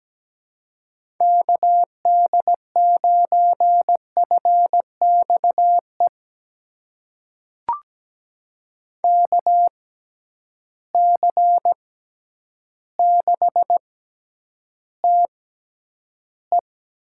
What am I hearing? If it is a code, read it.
Morse code, heard as a keyed tone, about 17 wpm, spaced as for 9 wpm, 700 Hz. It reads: KD9FXE KC6TE
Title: USA Call Signs - 6 Characters - 6x Character Spacing - 17wpm